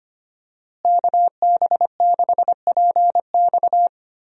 Morse code, heard as a keyed tone, about 25 wpm, 700 Hz. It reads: KB6PX
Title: USA Call Signs - 5 Characters - Repeated 3x - 25wpm